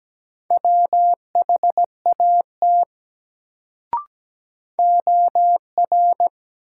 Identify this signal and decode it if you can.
Morse code, heard as a keyed tone, about 17 wpm, 700 Hz. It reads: WHAT OR
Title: Sentences from Top 100 Words - New Word Drill - 17wpm